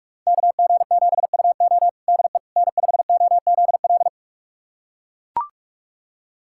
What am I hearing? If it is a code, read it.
Morse code, heard as a keyed tone, about 44 wpm, 700 Hz. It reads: KG7UO DE N5OZL
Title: USA Callsign DE Callsign - 5 Characters - 45wpm